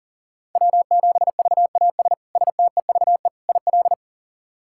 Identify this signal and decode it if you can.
Morse code, heard as a keyed tone, about 40 wpm, 700 Hz. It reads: W7VAS STEVE IL
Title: K1USN SST QSO Practice - 40wpm